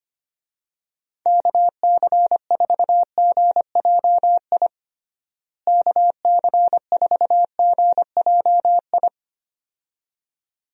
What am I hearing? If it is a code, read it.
Morse code, heard as a keyed tone, about 25 wpm, 700 Hz. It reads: KC4GJI KC4GJI
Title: USA Call Signs - 6 Characters - Repeated 2x - 25wpm